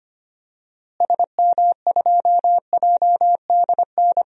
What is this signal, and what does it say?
Morse code, heard as a keyed tone, about 25 wpm, 700 Hz. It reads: SM2JDN
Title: DXCC-100 Call Signs - 6 Characters - Repeated 2x - 25wpm